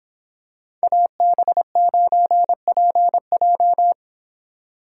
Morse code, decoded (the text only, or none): AB9PJ